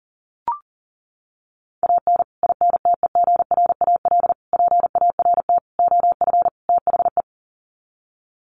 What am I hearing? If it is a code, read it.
Morse code, heard as a keyed tone, about 40 wpm, 700 Hz. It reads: AN INTEGRAL PART OF THE
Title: Sets of 5 Words 40wpm